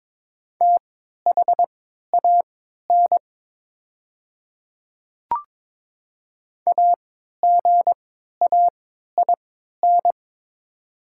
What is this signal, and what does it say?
Morse code, heard as a keyed tone, about 22 wpm, spaced as for 12 wpm, 700 Hz. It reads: THAN AGAIN
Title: 3x Character Spacing - Top 200 Words - 22wpm